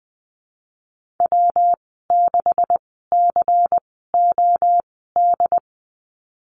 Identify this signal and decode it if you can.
Morse code, heard as a keyed tone, about 20 wpm, spaced as for 15 wpm, 700 Hz. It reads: W6COD
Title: 161 Lesson 26g - Letter g - Call Signs 20wpm